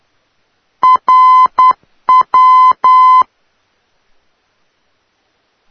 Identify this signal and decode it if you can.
Morse code, heard as a keyed tone, about 10 wpm, 1020 Hz. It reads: RW
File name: europe / d-4041-rw-392-tegel-k3-2014-05-rb